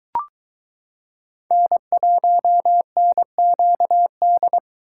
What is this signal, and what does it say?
Morse code, heard as a keyed tone, about 23 wpm, 700 Hz. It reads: N1NQD